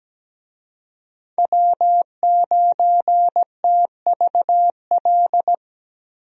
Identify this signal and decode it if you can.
Morse code, heard as a keyed tone, about 17 wpm, 700 Hz. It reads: W9TVL
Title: USA Call Signs - 5 Characters - Repeated 2x - 17wpm